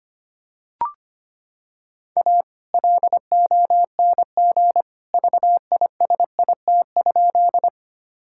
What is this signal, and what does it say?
Morse code, heard as a keyed tone, about 25 wpm, 700 Hz. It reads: A LONG VISIT?
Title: Sentences from Top 1000 Words - Review Part 6 - 25wpm